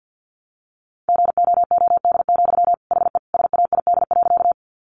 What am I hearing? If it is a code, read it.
Morse code, heard as a keyed tone, about 50 wpm, 700 Hz. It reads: GOOD, HE SAID.
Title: Sentences from Top 200 Words 50wpm